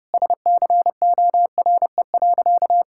Morse code, heard as a keyed tone, about 30 wpm, 700 Hz. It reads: SCORE.
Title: Sentences from Top 1600 Words 30wpm